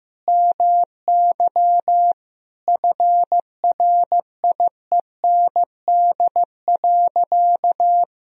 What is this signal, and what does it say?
Morse code, heard as a keyed tone, about 15 wpm, 700 Hz. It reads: MY FRIEND.
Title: Sentences from Top 1000 Words - Review Part 3 - 15wpm